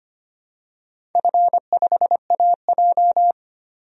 Morse code, heard as a keyed tone, about 25 wpm, 700 Hz. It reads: F5AJ
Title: DXCC-100 Call Signs - 4 Characters - 25wpm